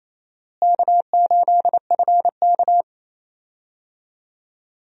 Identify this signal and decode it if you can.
Morse code, heard as a keyed tone, about 28 wpm, 700 Hz. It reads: K8FK
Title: USA Call Signs - 4 Characters - Repeated 2x - 28wpm